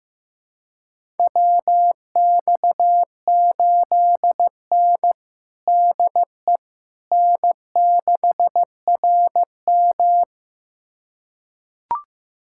Morse code, heard as a keyed tone, about 15 wpm, 700 Hz. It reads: WX8N DE N6RM
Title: USA Callsign DE Callsign - 4 Characters - 15wpm